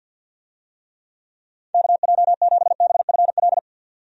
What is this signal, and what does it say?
Morse code, heard as a keyed tone, about 52 wpm, 700 Hz. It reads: KJ7BFL
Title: USA Call Signs - 6 Characters - 50wpm